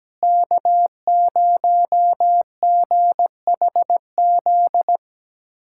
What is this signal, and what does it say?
Morse code, heard as a keyed tone, about 17 wpm, 700 Hz. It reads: K0GHZ
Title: USA Callsign DE Callsign - 5 Characters - 17wpm